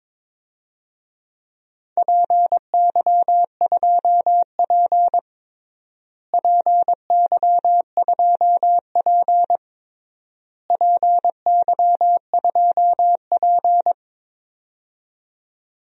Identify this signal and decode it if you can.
Morse code, heard as a keyed tone, about 22 wpm, 700 Hz. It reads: PY2P PY2P PY2P
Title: DXCC-100 Call Signs - 4 Characters - Repeated 3x - 22wpm